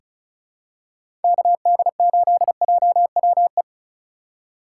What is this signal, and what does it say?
Morse code, heard as a keyed tone, about 36 wpm, 700 Hz. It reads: KD8JWE
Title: USA Call Signs - 2x Character Spacing - 35wpm